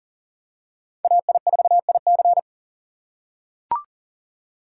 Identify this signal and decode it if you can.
Morse code, heard as a keyed tone, about 40 wpm, 700 Hz. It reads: AI4IC
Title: USA Call Signs - 5 Characters - 4x Character Spacing - 40wpm